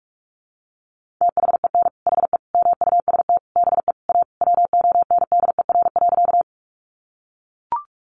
Morse code, heard as a keyed tone, about 44 wpm, 700 Hz. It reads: THEN HE MUST BE A WONDER.
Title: Sentences from Top 500 Words 45wpm